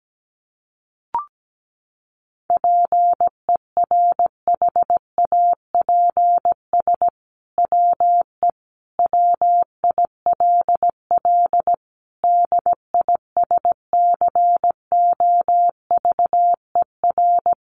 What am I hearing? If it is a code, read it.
Morse code, heard as a keyed tone, about 17 wpm, 700 Hz. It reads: PERHAPS WE WILL DISCOVER
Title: Sentences from Top 1100 Words 17wpm